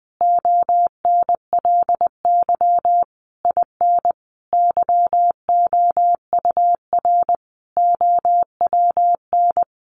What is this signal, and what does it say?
Morse code, heard as a keyed tone, about 20 wpm, 700 Hz. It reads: ONLY IN YOUR OWN
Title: Sentences from Top 1300 Words 20wpm